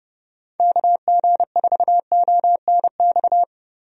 Morse code, heard as a keyed tone, about 30 wpm, 700 Hz. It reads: KG4ONX
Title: USA Call Signs - 6 Characters - Repeated 2x - 30wpm